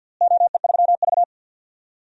Morse code, heard as a keyed tone, about 52 wpm, 700 Hz. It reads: OE3V
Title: DXCC-100 Call Signs - 4 Characters - Repeated 3x - 50wpm